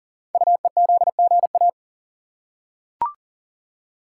Morse code, heard as a keyed tone, about 42 wpm, 690 Hz. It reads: UEZGA
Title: Code Groups - 5 Letter:Numbers - 40wpm